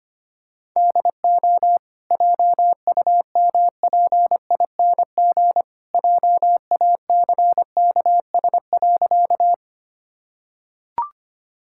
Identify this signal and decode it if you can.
Morse code, heard as a keyed tone, about 25 wpm, 700 Hz. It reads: DO JUMPING JACKS.
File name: Easy Sentences with Hints - 25wpm